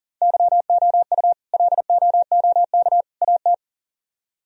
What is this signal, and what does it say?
Morse code, heard as a keyed tone, about 51 wpm, 700 Hz. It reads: YOU LOOK AT